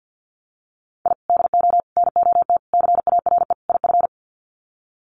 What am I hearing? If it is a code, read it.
Morse code, heard as a keyed tone, about 50 wpm, 700 Hz. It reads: I DO NOT CARE IF
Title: Sets of 5 Words 50wpm